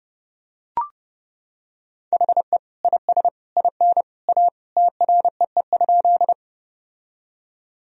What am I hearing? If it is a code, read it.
Morse code, heard as a keyed tone, about 32 wpm, 700 Hz. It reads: HE IS IN A TREE?
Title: Sentences from Top 1000 Words - Review Part 1 - 30wpm